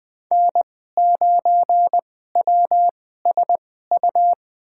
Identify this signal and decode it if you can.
Morse code, heard as a keyed tone, about 21 wpm, spaced as for 15 wpm, 700 Hz. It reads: N9WSU
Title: USA Call Signs - 5 Characters - 2x Character Spacing - 20wpm